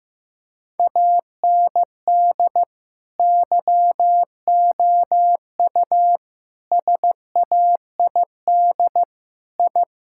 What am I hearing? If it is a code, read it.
Morse code, heard as a keyed tone, about 15 wpm, 700 Hz. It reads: AND YOU SAID I
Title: Sentences from Top 1000 Words - Review Part 5 - 15wpm